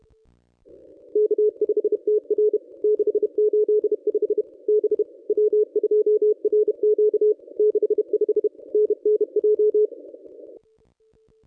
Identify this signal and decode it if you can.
Morse code, heard as a keyed tone, about 32 wpm, 420 Hz. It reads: K5TR 685 B W2RQ 65 NNJ
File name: contests / ss-cw-k5tr-op-n5rz -1000